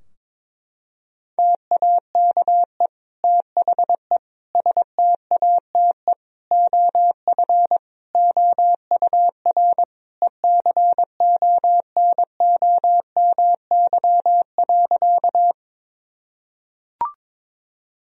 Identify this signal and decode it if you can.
Morse code, heard as a keyed tone, about 22 wpm, 700 Hz. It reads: TAKE THE STATE OF OUR ECONOMY.
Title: Sentences from Top 1000 Words - Review Part 3 - 22wpm